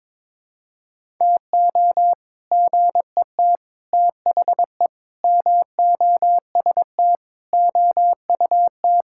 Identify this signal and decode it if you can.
Morse code, heard as a keyed tone, about 22 wpm, 700 Hz. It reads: TO GET THE MOST OUT
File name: Sentences with 7-Word Phrases - 22wpm